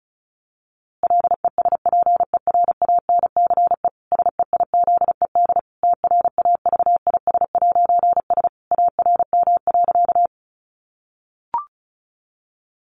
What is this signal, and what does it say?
Morse code, heard as a keyed tone, about 34 wpm, 700 Hz. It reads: LESPERANCE SEIZED TRAVIS'S ARM.
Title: A Sound Of Thunder 35wpm